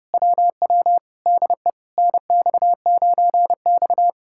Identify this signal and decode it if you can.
Morse code, heard as a keyed tone, about 33 wpm, 700 Hz. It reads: WW DE NX9X